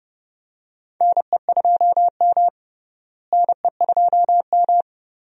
Morse code, heard as a keyed tone, about 30 wpm, 700 Hz. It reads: NE2M NE2M
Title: USA Call Signs - 4 Characters - Repeated 2x - 30wpm